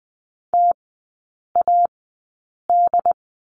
Morse code, heard as a keyed tone, about 20 wpm, spaced as for 8 wpm, 700 Hz. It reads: TAD
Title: 081 Lesson 14e - Number 2 - Three Characters 20wpm